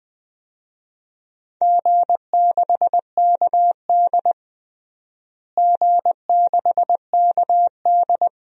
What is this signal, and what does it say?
Morse code, heard as a keyed tone, about 20 wpm, 700 Hz. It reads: G6KD G6KD
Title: DXCC-100 Call Signs - 4 Characters - Repeated 3x - 20wpm